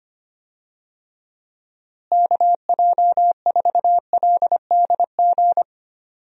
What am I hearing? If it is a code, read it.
Morse code, heard as a keyed tone, about 25 wpm, 700 Hz. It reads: KJ4LDG
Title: USA Call Signs - 6 Characters - Repeated 3x - 25wpm